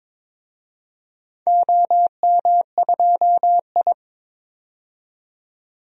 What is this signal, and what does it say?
Morse code, heard as a keyed tone, about 23 wpm, 700 Hz. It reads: OM2I
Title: DXCC-100 Call Signs - 4 Characters - 22wpm